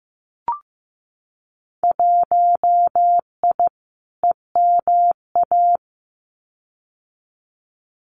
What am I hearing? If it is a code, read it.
Morse code, heard as a keyed tone, about 15 wpm, 700 Hz. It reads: 1I EMA
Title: Winter Field Day Exchanges Encore 15wpm